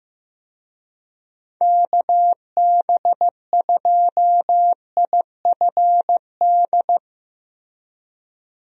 Morse code, heard as a keyed tone, about 15 wpm, 700 Hz. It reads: KB2IFD
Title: USA Call Signs - 6 Characters - Repeated 2x - 15wpm